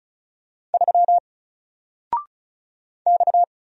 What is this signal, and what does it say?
Morse code, heard as a keyed tone, about 36 wpm, 700 Hz. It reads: 3 X
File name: Single-Letter-Number-35wpm